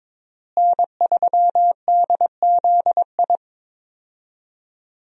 Morse code, heard as a keyed tone, about 22 wpm, 700 Hz. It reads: N3DZI
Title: USA Call Signs - 5 Characters - Repeated 2x - 22wpm